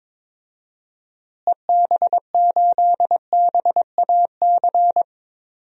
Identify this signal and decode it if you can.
Morse code, heard as a keyed tone, about 22 wpm, 700 Hz. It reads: EB8BAC
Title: DXCC-100 Call Signs - 6 Characters - Repeated 2x - 22wpm